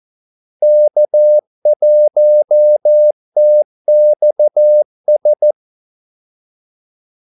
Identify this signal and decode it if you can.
Morse code, heard as a keyed tone, about 14 wpm, 600 Hz. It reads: K1TXS
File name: Callsigns-14wpm